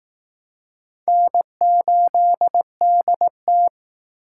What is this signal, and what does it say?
Morse code, heard as a keyed tone, about 18 wpm, 700 Hz. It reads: N8DT
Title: Speed Racing - USA 1x2 Call Signs 15wpm